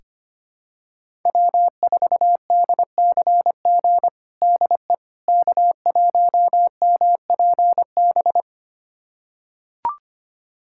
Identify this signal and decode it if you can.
Morse code, heard as a keyed tone, about 25 wpm, 700 Hz. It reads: W4DCG DE K1MPB
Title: USA Callsign DE Callsign - 5 Characters - 25wpm